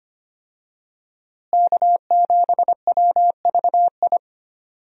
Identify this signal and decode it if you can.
Morse code, heard as a keyed tone, about 25 wpm, 700 Hz. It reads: K7WVI
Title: USA Call Signs - 5 Characters - Repeated 2x - 25wpm